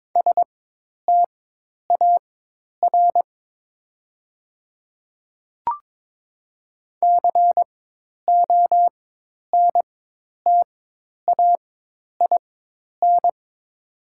Morse code, heard as a keyed tone, about 22 wpm, spaced as for 10 wpm, 700 Hz. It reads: STAR CONTAIN
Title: 4x Character Spacing - Top 500 Words - 22wpm